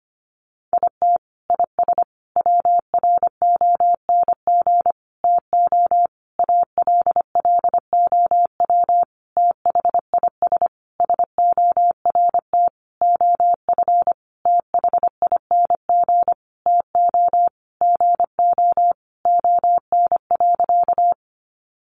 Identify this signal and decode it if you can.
Morse code, heard as a keyed tone, about 25 wpm, 700 Hz. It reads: IT IS WRONG TO ALLOW THIS SORT OF THING TO GO ON.
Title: Sentences from Top 1300 Words 25wpm